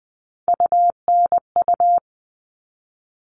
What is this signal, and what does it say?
Morse code, heard as a keyed tone, about 20 wpm, 700 Hz. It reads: UNU
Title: All Cut Numbers - 3-Digit 20wpm